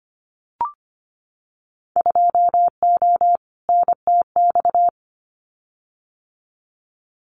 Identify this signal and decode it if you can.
Morse code, heard as a keyed tone, about 25 wpm, 700 Hz. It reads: 2O NTX
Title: Winter Field Day Exchanges Encore 25wpm